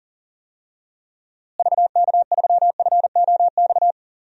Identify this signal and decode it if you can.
Morse code, heard as a keyed tone, about 40 wpm, 700 Hz. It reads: VK3FOX
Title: DXCC-100 Call Signs - 6 Characters - Repeated 3x - 40wpm